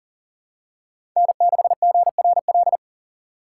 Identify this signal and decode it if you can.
Morse code, heard as a keyed tone, about 40 wpm, 700 Hz. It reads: N6GRL
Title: USA Call Signs - 5 Characters - 4x Character Spacing - 40wpm